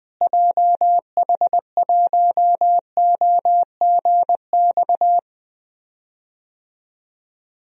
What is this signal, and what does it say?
Morse code, heard as a keyed tone, about 20 wpm, 700 Hz. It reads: JH1OGX
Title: DXCC-100 Call Signs - 6 Characters - Repeated 3x - 20wpm